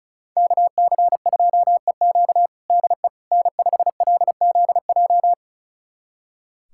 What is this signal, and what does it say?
Morse code, heard as a keyed tone, about 34 wpm, 700 Hz. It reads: KC2EQ DE N5LZJ
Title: USA Callsign DE Callsign - 5 Characters - 35wpm